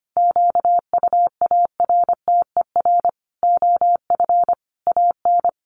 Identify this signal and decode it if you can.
Morse code, heard as a keyed tone, about 25 wpm, 700 Hz. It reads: QUARTER OF AN
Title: Sentences from Top 1300 Words 25wpm